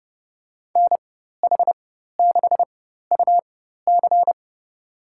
Morse code, heard as a keyed tone, about 30 wpm, spaced as for 14 wpm, 700 Hz. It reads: NH6UC
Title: USA Call Signs - 5 Characters - 4x Character Spacing - 30wpm